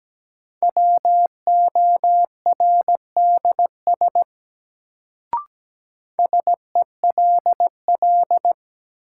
Morse code, heard as a keyed tone, about 17 wpm, 700 Hz. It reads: WORDS SELL
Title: Sentences from Top 1000 Words - Review Part 6 - New Word Drill - 17wpm